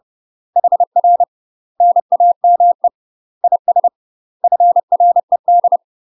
Morse code, heard as a keyed tone, about 30 wpm, 700 Hz. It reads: HR NAME IS FRED
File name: UK MASTER QSOs TRAINING / 30WPM / UK MASTER QSOs 0049